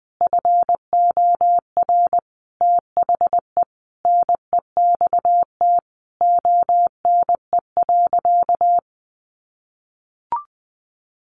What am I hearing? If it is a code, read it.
Morse code, heard as a keyed tone, about 20 wpm, 700 Hz. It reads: FOR THE NEXT ONE.
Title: Sentences from Top 400 Words 20wpm